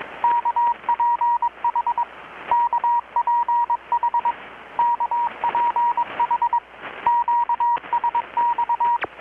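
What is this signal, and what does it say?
Morse code, heard as a keyed tone, about 21 wpm, 950 Hz, with radio noise. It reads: KPH KPH KPH QSX
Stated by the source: Morse from KPH's Night of Nights XX maritime radiotelegraphy event, received in Chiapas, Mexico about 3,800 km away on a Kenwood R-600 with a grounded 10 m long-wire antenna.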